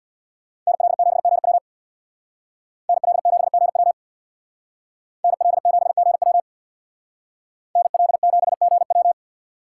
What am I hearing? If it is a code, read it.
Morse code, heard as a keyed tone, about 63 wpm, 700 Hz. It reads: NL7GW NL7GW NL7GW NL7GW